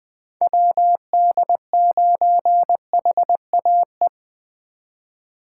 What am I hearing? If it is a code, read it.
Morse code, heard as a keyed tone, about 24 wpm, 700 Hz. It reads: WD9HAE
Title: Speed Racing - USA 2x3 Call Signs 20wpm